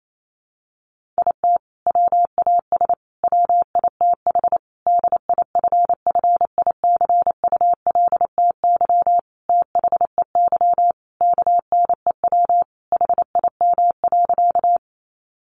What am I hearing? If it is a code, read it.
Morse code, heard as a keyed tone, about 28 wpm, 700 Hz. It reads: IT WAS WITH DIFFICULTY THEY KNEW HIM.
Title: Sentences from Top 1500 Words 28wpm